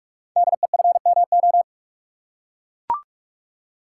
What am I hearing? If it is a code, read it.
Morse code, heard as a keyed tone, about 44 wpm, 700 Hz. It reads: DEFMO
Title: Code Groups - 5 Letters - 45wpm